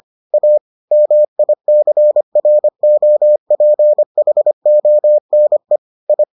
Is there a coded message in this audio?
Yes: Morse code from 0.3 to 6.2 s, heard as a keyed tone, about 25 wpm, 600 Hz. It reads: A MICROPHONE I